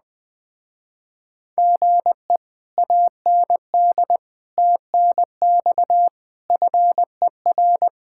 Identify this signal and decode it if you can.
Morse code, heard as a keyed tone, about 20 wpm, 700 Hz. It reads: GE AND TNX FER
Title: Speed-Racing Common QSO Phrases - 17wpm